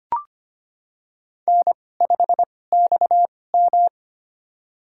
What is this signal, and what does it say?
Morse code, heard as a keyed tone, about 25 wpm, spaced as for 18 wpm, 700 Hz. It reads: N5XM
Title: USA Call Signs - 2x Character Spacing - 25wpm